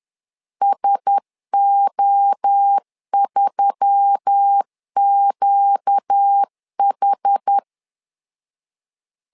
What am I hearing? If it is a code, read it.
Morse code, heard as a keyed tone, about 10 wpm, 800 Hz. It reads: SO3QH
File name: LEZIONI / 4 Q1S3 10wpm 2427